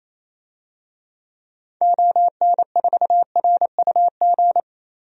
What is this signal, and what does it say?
Morse code, heard as a keyed tone, about 28 wpm, 700 Hz. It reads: ON4RUG
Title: DXCC-100 Call Signs - 6 Characters - 28wpm